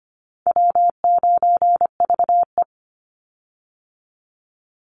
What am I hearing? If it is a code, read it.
Morse code, heard as a keyed tone, about 25 wpm, 700 Hz. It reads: W9VE
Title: Easy Call Signs Repeated 2x - 25wpm